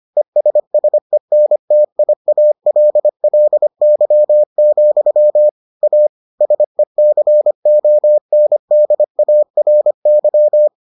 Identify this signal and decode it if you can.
Morse code, heard as a keyed tone, about 25 wpm, 600 Hz. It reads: ESSENTIALLY, A SECONDARY